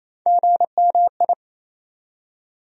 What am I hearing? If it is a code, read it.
Morse code, heard as a keyed tone, about 28 wpm, 700 Hz. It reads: GMI
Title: USA Call Sign Postfixes - 3 Letters 28wpm